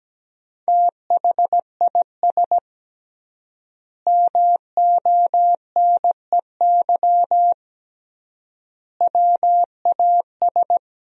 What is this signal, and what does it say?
Morse code, heard as a keyed tone, about 17 wpm, 700 Hz. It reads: THIS MONEY WAS
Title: 3x Word-Spacing - Sentences from Top 400 Words - 17wpm